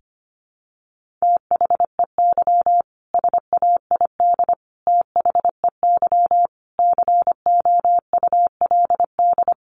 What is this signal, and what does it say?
Morse code, heard as a keyed tone, about 25 wpm, 700 Hz. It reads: THEY SAID THEY COULD
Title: Sentences from Top 1400 Words 25wpm